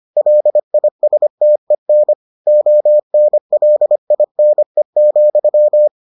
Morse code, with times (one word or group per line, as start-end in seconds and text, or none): 0.2-2.1 LISTEN
2.5-5.9 ONLINE,